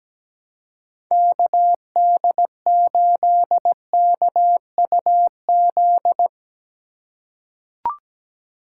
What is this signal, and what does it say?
Morse code, heard as a keyed tone, about 17 wpm, 700 Hz. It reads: KD8KUZ
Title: USA Call Signs - 6 Characters - 4x Character Spacing - 17wpm